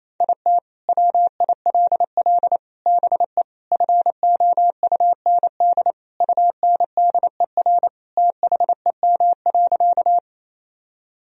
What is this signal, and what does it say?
Morse code, heard as a keyed tone, about 28 wpm, 700 Hz. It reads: IT WILL BE FOUND UNDER THEM.
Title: Sentences from Top 1000 Words - Review Part 1 - 28wpm